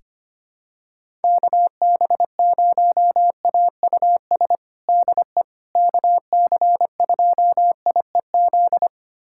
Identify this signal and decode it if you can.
Morse code, heard as a keyed tone, about 25 wpm, 700 Hz. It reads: KB0AUS DE KC2IEZ
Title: USA Callsign DE Callsign 25wpm